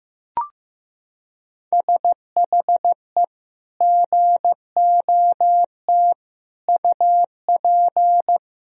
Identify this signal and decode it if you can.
Morse code, heard as a keyed tone, about 16 wpm, 700 Hz. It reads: SHE GOT UP
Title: Sentences from Top 1000 Words - Review Part 3 - 15wpm